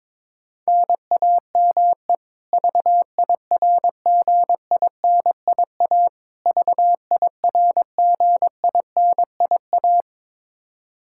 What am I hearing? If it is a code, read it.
Morse code, heard as a keyed tone, about 22 wpm, 700 Hz. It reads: NAME VIRGINIA VIRGINIA
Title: Practice QSO - Exchange 1 - Beginning - 22wpm